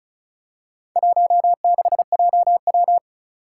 Morse code, heard as a keyed tone, about 36 wpm, 700 Hz. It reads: 16JW
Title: Four Challenging Letter-Number Combinations 35wpm